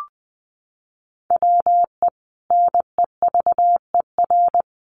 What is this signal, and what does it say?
Morse code, heard as a keyed tone, about 19 wpm, 700 Hz. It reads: WE NEVER
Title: Sentences from Top 1000 Words 20wpm